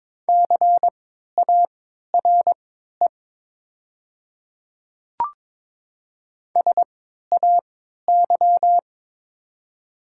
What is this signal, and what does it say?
Morse code, heard as a keyed tone, about 22 wpm, spaced as for 12 wpm, 700 Hz. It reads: CARE SAY
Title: 3x Character Spacing - Top 300 Words - 22wpm